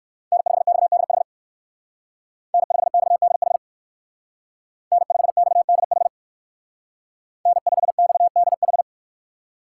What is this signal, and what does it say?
Morse code, heard as a keyed tone, about 56 wpm, 700 Hz. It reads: N5XDH N5XDH N5XDH N5XDH
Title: Speed Racing - USA 1x3 Call Signs 45wpm